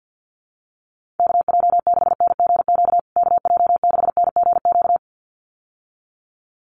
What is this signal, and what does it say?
Morse code, heard as a keyed tone, about 50 wpm, 700 Hz. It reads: KJ6NGQ KJ6NGQ
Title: USA Call Signs Repeated 2x 50wpm